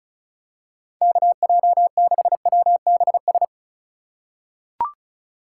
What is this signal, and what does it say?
Morse code, heard as a keyed tone, about 36 wpm, 700 Hz. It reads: KJ6WBS
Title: USA Call Signs - 6 Characters - 3x Character Spacing - 35wpm